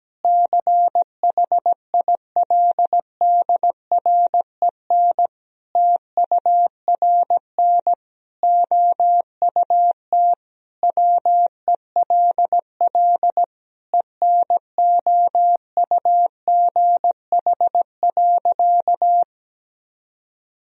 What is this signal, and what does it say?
Morse code, heard as a keyed tone, about 17 wpm, 700 Hz. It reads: CHILDREN TURN OUT WELL ENOUGH.
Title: Sentences from Top 1000 Words - Review Part 3 - 17wpm